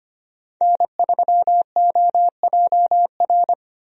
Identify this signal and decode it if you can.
Morse code, heard as a keyed tone, about 25 wpm, 700 Hz. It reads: N3OJR
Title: USA Call Signs - 5 Characters - Repeated 2x - 25wpm